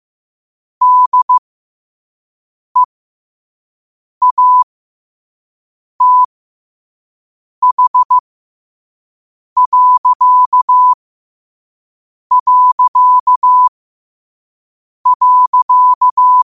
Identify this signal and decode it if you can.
Morse code, heard as a keyed tone, about 15 wpm, spaced as for 5 wpm, 1000 Hz. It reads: DEATH...